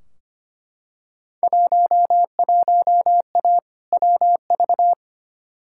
Morse code, heard as a keyed tone, about 25 wpm, 700 Hz. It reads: 11A WV
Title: Field Day Exchanges Encore - 25wpm